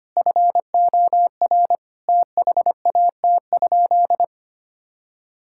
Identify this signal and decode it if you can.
Morse code, heard as a keyed tone, about 25 wpm, 700 Hz. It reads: FOR THAT?
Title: Sentences from Top 1000 Words - Review Part 5 - 25wpm